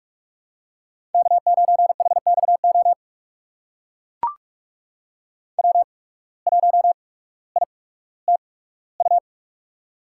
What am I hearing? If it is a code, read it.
Morse code, heard as a keyed tone, about 45 wpm, spaced as for 19 wpm, 700 Hz. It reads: K9HXO W1ITU
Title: USA Call Signs - 5 Characters - 8x Character Spacing - 45wpm